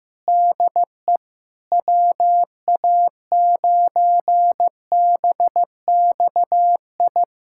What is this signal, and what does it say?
Morse code, heard as a keyed tone, about 14 wpm, 700 Hz. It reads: DE WA9BXI
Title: K1USN SST QSO Practice 15wpm